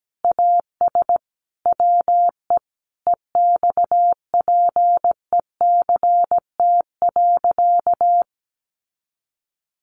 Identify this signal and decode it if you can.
Morse code, heard as a keyed tone, about 17 wpm, 700 Hz. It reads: AS WE EXPECT.
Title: Sentences from Top 1100 Words 17wpm